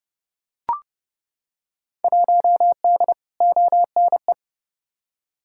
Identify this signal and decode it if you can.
Morse code, heard as a keyed tone, about 32 wpm, 700 Hz. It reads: 1D ONE
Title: Field Day Exchanges - 30wpm